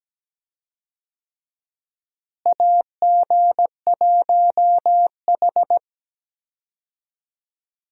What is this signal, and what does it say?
Morse code, heard as a keyed tone, about 17 wpm, 700 Hz. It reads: AG1H
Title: USA Call Signs - 4 Characters - Repeated 2x - 17wpm